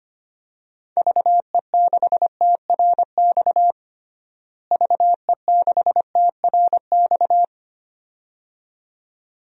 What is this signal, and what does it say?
Morse code, heard as a keyed tone, about 25 wpm, 700 Hz. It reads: VE6TRX VE6TRX
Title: DXCC-100 Call Signs - 6 Characters - Repeated 2x - 25wpm